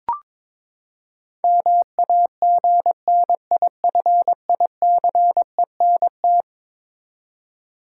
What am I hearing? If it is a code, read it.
Morse code, heard as a keyed tone, about 21 wpm, 700 Hz. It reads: MAGNIFICENT
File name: practice / 22wpm / Eleven Letter Words - 22wpm